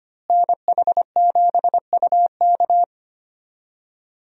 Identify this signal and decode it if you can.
Morse code, heard as a keyed tone, about 25 wpm, 700 Hz. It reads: NH7UK
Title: USA Call Signs - 5 Characters - Repeated 2x - 25wpm